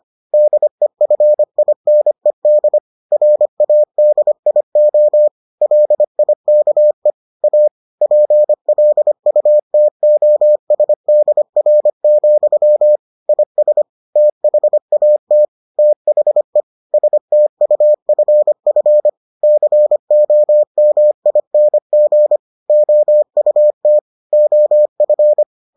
Morse code, heard as a keyed tone, about 25 wpm, 600 Hz. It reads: DEFINED RADIO LIKE A PLUTOSDR, IS THAT THE STUFF COMING OUT OF